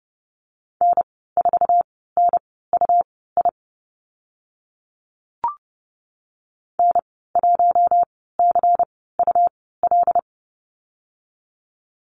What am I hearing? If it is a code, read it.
Morse code, heard as a keyed tone, about 30 wpm, spaced as for 17 wpm, 700 Hz. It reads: N4NUI N1CUL
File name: Lesson 19g - Letter w - Call Signs 30wpm